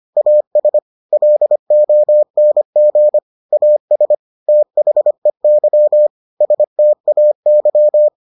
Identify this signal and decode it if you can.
Morse code, heard as a keyed tone, about 25 wpm, 600 Hz. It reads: AS LONG AS THEY STAY